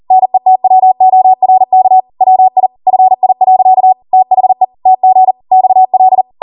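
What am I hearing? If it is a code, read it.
Morse code, heard as a keyed tone, about 40 wpm, 750 Hz. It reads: NETWORK WI FI. THE TG XL
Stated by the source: ARRL 40 WPM Morse code practice transmission.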